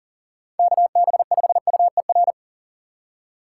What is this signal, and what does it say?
Morse code, heard as a keyed tone, about 50 wpm, 700 Hz. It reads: KB5UER
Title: Speed Racing - USA 2x3 Call Signs 40wpm